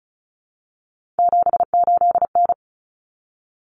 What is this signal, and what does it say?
Morse code, heard as a keyed tone, about 34 wpm, 700 Hz. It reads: 78N
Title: Common Cut Numbers - 3-Digit 35wpm